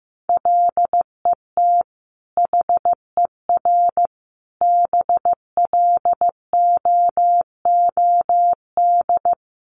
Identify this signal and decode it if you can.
Morse code, heard as a keyed tone, about 15 wpm, 700 Hz. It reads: LET HER BLOOD
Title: Sentences from Top 1200 Words 15wpm